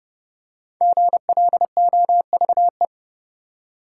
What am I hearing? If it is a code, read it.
Morse code, heard as a keyed tone, about 30 wpm, 700 Hz. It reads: GLOVE
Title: Five Letter Words - 30wpm